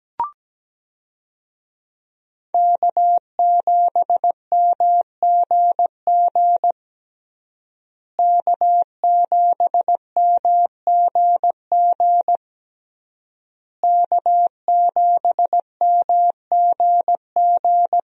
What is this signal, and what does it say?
Morse code, heard as a keyed tone, about 17 wpm, 700 Hz. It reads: K7MGG K7MGG K7MGG
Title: USA Call Signs - 5 Characters - Repeated 3x - 17wpm